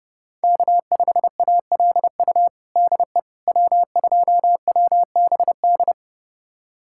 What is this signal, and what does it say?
Morse code, heard as a keyed tone, about 32 wpm, 700 Hz. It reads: K5ALU DE W2WBD
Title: USA Callsign DE Callsign - 5 Characters - 30wpm